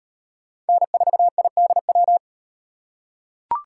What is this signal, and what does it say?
Morse code, heard as a keyed tone, about 38 wpm, 700 Hz. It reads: N4IDW